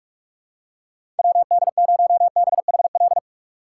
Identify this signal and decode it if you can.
Morse code, heard as a keyed tone, about 41 wpm, 700 Hz. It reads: WD0BHL
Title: USA Call Signs - 3x Character Spacing - 45wpm